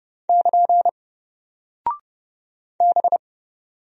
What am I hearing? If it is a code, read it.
Morse code, heard as a keyed tone, about 30 wpm, 700 Hz. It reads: <KN> B
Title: Single Character (Letter:Number:Punctuation:Prosign) Alt 30wpm